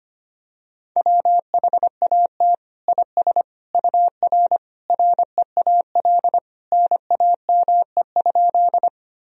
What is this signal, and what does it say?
Morse code, heard as a keyed tone, about 25 wpm, 700 Hz. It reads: WHAT IS UR REAL NAME?
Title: Sentences from Top 300 Words and Abbreviations 25wpm